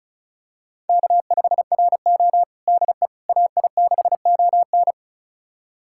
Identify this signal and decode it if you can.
Morse code, heard as a keyed tone, about 34 wpm, 700 Hz. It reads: K5RO DE AI6ON
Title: USA Callsign DE Callsign 35wpm